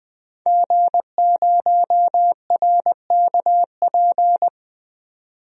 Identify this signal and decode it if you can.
Morse code, heard as a keyed tone, about 20 wpm, 700 Hz. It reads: G0RKP
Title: DXCC-100 Call Signs - 5 Characters - 20wpm